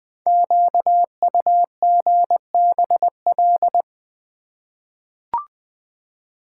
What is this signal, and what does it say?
Morse code, heard as a keyed tone, about 20 wpm, 700 Hz. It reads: QUGBL
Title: Code Groups - 5 Letters - 20wpm